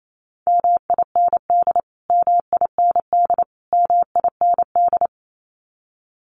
Morse code, heard as a keyed tone, about 28 wpm, 700 Hz. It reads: MIND MIND MIND
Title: Top 500 Words Repeated 28wpm